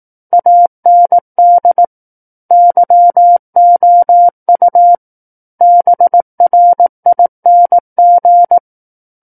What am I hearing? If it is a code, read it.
Morse code, heard as a keyed tone, about 18 wpm, 700 Hz. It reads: AND YOU BRING
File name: 18 WPM